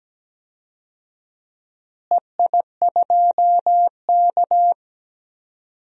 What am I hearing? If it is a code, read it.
Morse code, heard as a keyed tone, about 17 wpm, 700 Hz. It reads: EI2K
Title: DXCC-100 Call Signs - 4 Characters - Repeated 3x - 17wpm